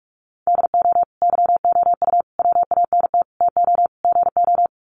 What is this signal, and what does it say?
Morse code, heard as a keyed tone, about 44 wpm, 690 Hz. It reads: DO YOU WANT TO GO
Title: Sets of 5 Words Encore2 45wpm